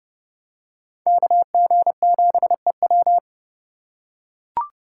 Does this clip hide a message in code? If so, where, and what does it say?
Yes: Morse code from 1.1 to 3.2 s, heard as a keyed tone, about 26 wpm, 700 Hz. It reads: KG7EW